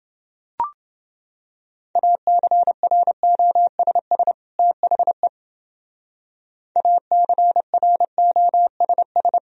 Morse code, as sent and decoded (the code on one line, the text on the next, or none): .- -.-. .-. --- ... ... / - .... / .- -.-. .-. --- ... ...
ACROSS THE ACROSS